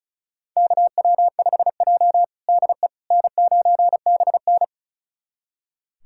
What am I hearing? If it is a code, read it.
Morse code, heard as a keyed tone, about 36 wpm, 700 Hz. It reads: KW5J DE N9BN
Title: USA Callsign DE Callsign - 4 Characters - 35wpm